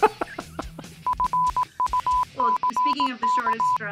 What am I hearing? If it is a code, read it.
Morse code, heard as a keyed tone, about 18 wpm, 1000 Hz, with speech. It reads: FUCK